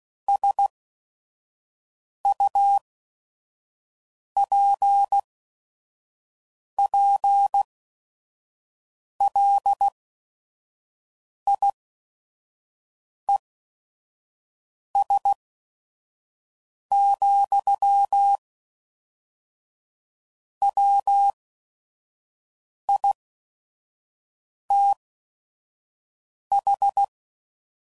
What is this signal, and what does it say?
Morse code, heard as a keyed tone, about 16 wpm, spaced as for 5 wpm, 800 Hz. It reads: SUPPLIES, WITH
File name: CW Code Groups @5WPM